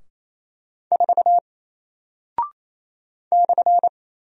Morse code, heard as a keyed tone, about 28 wpm, 700 Hz. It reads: 4 /
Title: Single Character (Letter:Number:Punctuation:Prosign) Alt 28wpm